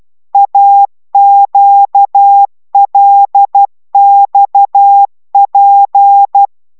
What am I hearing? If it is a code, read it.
Morse code, heard as a keyed tone, about 12 wpm, 800 Hz. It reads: AQLXP
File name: taviro tanulas / cw 69 16 10